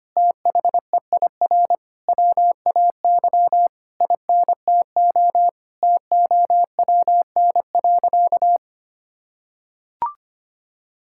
Morse code, heard as a keyed tone, about 25 wpm, 700 Hz. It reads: THEIR WAY INTO TOWN.
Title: Sentences from Top 1000 Words - Review Part 6 - 25wpm